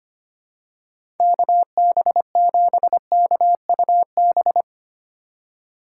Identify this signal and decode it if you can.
Morse code, heard as a keyed tone, about 25 wpm, 700 Hz. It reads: KB7KUB
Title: USA Call Signs - 4x Character Spacing - 25wpm